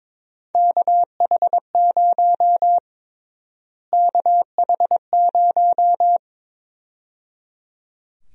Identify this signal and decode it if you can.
Morse code, heard as a keyed tone, about 22 wpm, 700 Hz. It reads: KH0 KH0
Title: USA Call Sign Prefixes - Repeated 2x - 22wpm